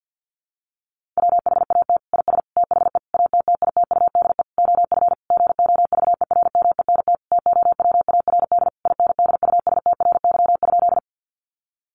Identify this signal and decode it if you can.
Morse code, heard as a keyed tone, about 48 wpm, 700 Hz. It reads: WHAT IS THE ATTITUDE OF GOVERMENT TOWARD INDUSTRY?
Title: Sentences from Top 1300 Words 50wpm